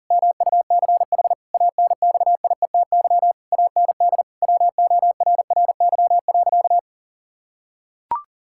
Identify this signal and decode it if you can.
Morse code, heard as a keyed tone, about 40 wpm, 700 Hz. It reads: MUCH ANXIETY AND WORRY.
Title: Pangram Challenge v2 - 40wpm